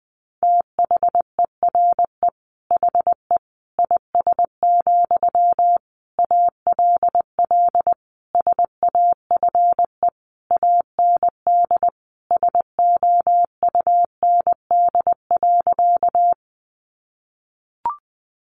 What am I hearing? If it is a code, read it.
Morse code, heard as a keyed tone, about 20 wpm, 700 Hz. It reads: THERE HE IS, ALL SAFE AND SOUND.
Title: Sentences from Top 900 Words 20wpm